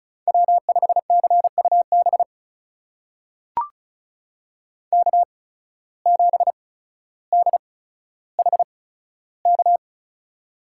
Morse code, heard as a keyed tone, about 36 wpm, spaced as for 15 wpm, 700 Hz. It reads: W5CUB K7DHK
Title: USA Call Signs - 5 Characters - 8x Character Spacing - 35wpm